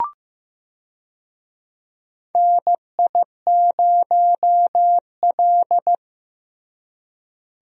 Morse code, heard as a keyed tone, about 15 wpm, 700 Hz. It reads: NI0L
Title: USA Call Signs - 4 Characters - Repeated 2x - 15wpm